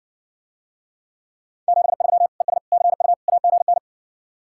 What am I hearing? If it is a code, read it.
Morse code, heard as a keyed tone, about 70 wpm, 700 Hz. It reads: 73 ES CU AGN